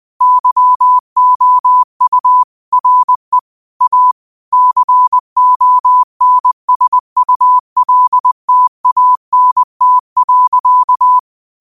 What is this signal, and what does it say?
Morse code, heard as a keyed tone, about 20 wpm, 1000 Hz. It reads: YOU RE A CONSULTANT.
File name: qotdfiles / 20wpm